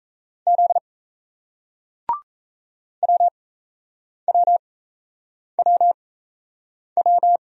Morse code, heard as a keyed tone, about 38 wpm, spaced as for 8 wpm, 700 Hz. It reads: Z WWWW